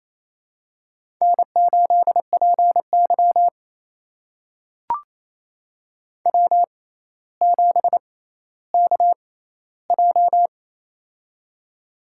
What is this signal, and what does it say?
Morse code, heard as a keyed tone, about 28 wpm, spaced as for 14 wpm, 700 Hz. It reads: N8PY W7KJ
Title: USA Call Signs - 4 Characters - 6x Character Spacing - 28wpm